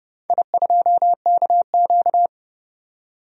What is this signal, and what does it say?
Morse code, heard as a keyed tone, about 29 wpm, 700 Hz. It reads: I2KQ
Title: DXCC-100 Call Signs - 4 Characters - 30wpm